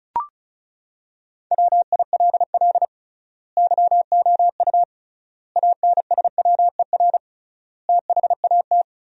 Morse code, heard as a keyed tone, about 36 wpm, 700 Hz. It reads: WILL YOU ANSWER THAT
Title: 3x Word-Spacing - Sentences from Top 400 Words - 35wpm